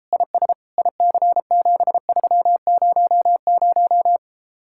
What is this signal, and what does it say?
Morse code, heard as a keyed tone, about 33 wpm, 700 Hz. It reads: IS IC7300